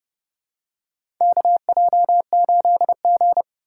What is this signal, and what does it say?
Morse code, heard as a keyed tone, about 30 wpm, 700 Hz. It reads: KJ8G
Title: USA Call Signs - 4 Characters - 4x Character Spacing - 30wpm